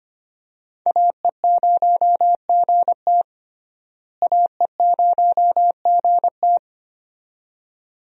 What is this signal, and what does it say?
Morse code, heard as a keyed tone, about 25 wpm, 700 Hz. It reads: AE0GT AE0GT
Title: USA Call Signs - 5 Characters - Repeated 2x - 25wpm